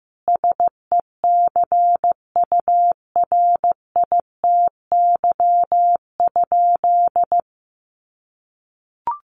Morse code, heard as a keyed tone, about 15 wpm, 700 Hz. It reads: SECURITY?
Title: Sentences from Top 1400 Words 15wpm